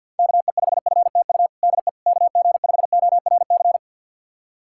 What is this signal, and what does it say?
Morse code, heard as a keyed tone, about 49 wpm, 700 Hz. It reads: KE5FTU DE KG5ORC
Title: USA Callsign DE Callsign 50wpm